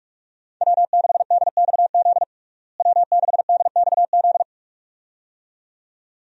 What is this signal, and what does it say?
Morse code, heard as a keyed tone, about 42 wpm, 700 Hz. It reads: W6DXZ W6DXZ
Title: USA Call Signs - 5 Characters - Repeated 2x - 45wpm